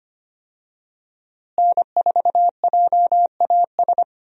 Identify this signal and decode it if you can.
Morse code, heard as a keyed tone, about 25 wpm, 700 Hz. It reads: N4JAS
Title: USA Call Signs - 5 Characters - Repeated 3x - 25wpm